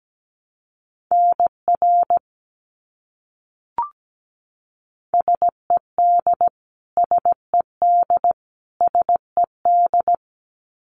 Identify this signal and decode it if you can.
Morse code, heard as a keyed tone, about 17 wpm, 700 Hz. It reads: NR SED SED SED
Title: Abbreviations Repeated 3x - 17wpm